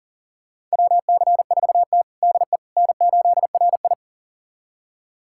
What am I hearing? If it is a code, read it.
Morse code, heard as a keyed tone, about 40 wpm, 700 Hz. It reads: WC4T DE N8RI
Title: USA Callsign DE Callsign - 4 Characters - 40wpm